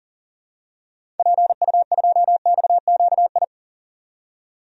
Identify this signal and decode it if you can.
Morse code, heard as a keyed tone, about 40 wpm, 700 Hz. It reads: PU2XQI
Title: DXCC-100 Call Signs - 6 Characters - 40wpm